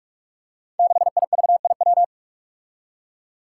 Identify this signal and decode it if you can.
Morse code, heard as a keyed tone, about 44 wpm, 700 Hz. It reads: 6IVIW
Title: Five Challenging Letter-Number Combinations 45wpm